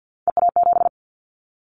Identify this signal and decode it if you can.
Morse code, heard as a keyed tone, about 50 wpm, 700 Hz. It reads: EA7
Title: Callsign Prefixes for Top 25 DXCC - 50wpm